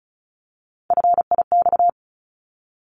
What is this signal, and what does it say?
Morse code, heard as a keyed tone, about 34 wpm, 700 Hz. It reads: FIX
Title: ED Suffix - Top 100 Words with Base Words 35wpm